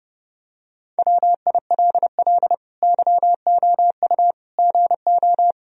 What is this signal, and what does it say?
Morse code, heard as a keyed tone, about 30 wpm, 700 Hz. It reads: WILL YOU GO
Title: Sentences from Top 1000 Words - Review Part 2 - 30wpm